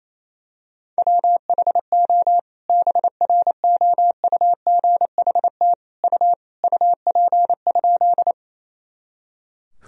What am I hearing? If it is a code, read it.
Morse code, heard as a keyed tone, about 28 wpm, 700 Hz. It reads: WHO BROUGHT U UP?
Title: Sentences from Top 500 Words and Abbreviations 28wpm